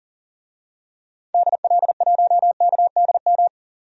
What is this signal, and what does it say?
Morse code, heard as a keyed tone, about 40 wpm, 700 Hz. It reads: DL1KDM
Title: DXCC-100 Call Signs - 6 Characters - 40wpm